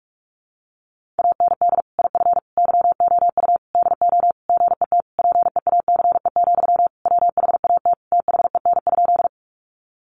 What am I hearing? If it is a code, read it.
Morse code, heard as a keyed tone, about 44 wpm, 700 Hz. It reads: AND IF YOU DO GET PEACE, WHAT THEN?
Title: Sentences from Top 1500 Words 45wpm